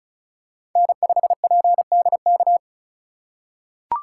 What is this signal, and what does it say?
Morse code, heard as a keyed tone, about 34 wpm, 700 Hz. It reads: N5PDK
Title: USA Call Signs - 5 Characters - 4x Character Spacing - 35wpm